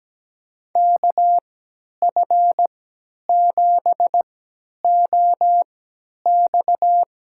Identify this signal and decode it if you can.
Morse code, heard as a keyed tone, about 17 wpm, spaced as for 10 wpm, 700 Hz. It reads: KF7OX
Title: USA Call Signs - 3x Character Spacing - 17wpm